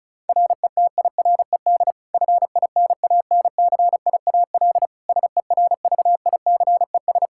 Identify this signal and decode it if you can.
Morse code, heard as a keyed tone, about 36 wpm, 700 Hz. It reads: RETIRED FINANCIAL SERVICES
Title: K1USN SST QSO Practice 35wpm